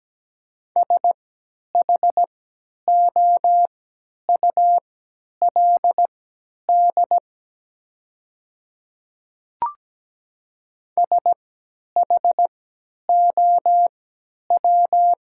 Morse code, heard as a keyed tone, about 17 wpm, spaced as for 10 wpm, 700 Hz. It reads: SHOULD SHOW
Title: 3x Character Spacing - Top 300 Words - 17wpm